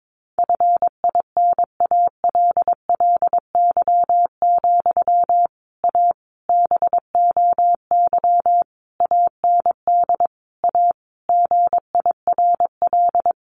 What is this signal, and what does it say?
Morse code, heard as a keyed tone, about 22 wpm, 700 Hz. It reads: FINALLY, A BOY AND A GIRL
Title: Sentences from Top 1200 Words 22wpm